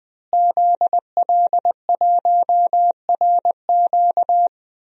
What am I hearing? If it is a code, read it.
Morse code, heard as a keyed tone, about 20 wpm, 700 Hz. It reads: ZL1RQ
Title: DXCC-100 Call Signs - 5 Characters - Repeated 2x - 20wpm